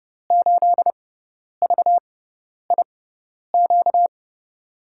Morse code, heard as a keyed tone, about 30 wpm, spaced as for 10 wpm, 700 Hz. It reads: 8VIQ
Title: USA Call Signs - 6 Characters - 6x Character Spacing - 30wpm